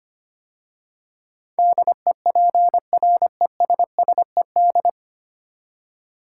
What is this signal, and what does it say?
Morse code, heard as a keyed tone, about 25 wpm, 700 Hz. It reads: DEPRESSED
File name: 9 Letter Words - First 3 Letters Given - 25wpm